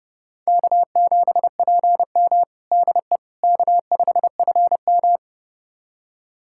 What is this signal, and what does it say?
Morse code, heard as a keyed tone, about 30 wpm, 700 Hz. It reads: K7PM DE K5FM
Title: USA Callsign DE Callsign - 4 Characters - 30wpm